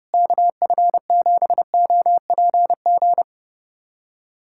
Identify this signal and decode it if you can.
Morse code, heard as a keyed tone, about 32 wpm, 700 Hz. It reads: KF7OPG
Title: USA Callsign DE Callsign - 6 Characters - 30wpm